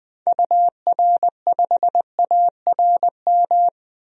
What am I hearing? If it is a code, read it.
Morse code, heard as a keyed tone, about 20 wpm, 700 Hz. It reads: UR5ARM
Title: DXCC-100 Call Signs - 6 Characters - 20wpm